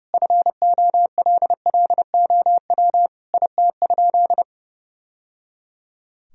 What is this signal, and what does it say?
Morse code, heard as a keyed tone, about 30 wpm, 700 Hz. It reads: FOLLOW IT?
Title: Sentences from Top 1000 Words - Review Part 1 - 30wpm